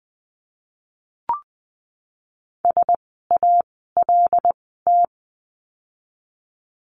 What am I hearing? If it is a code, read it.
Morse code, heard as a keyed tone, about 20 wpm, spaced as for 15 wpm, 700 Hz. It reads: SALT
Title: 075 Lesson 13f - Letter L - Words 20wpm